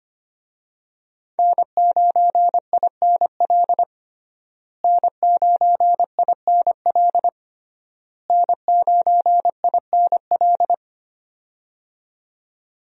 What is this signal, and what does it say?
Morse code, heard as a keyed tone, about 25 wpm, 700 Hz. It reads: N9INL N9INL N9INL
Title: USA Call Signs - 5 Characters - Repeated 3x - 25wpm